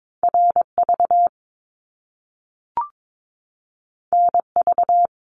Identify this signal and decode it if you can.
Morse code, heard as a keyed tone, about 22 wpm, 700 Hz. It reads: RV NV
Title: Top Two Letter Combinations in Words 22wpm